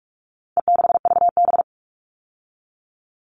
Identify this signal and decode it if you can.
Morse code, heard as a keyed tone, about 44 wpm, 700 Hz. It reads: E6VB
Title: All Cut Numbers - 4-Digit 45wpm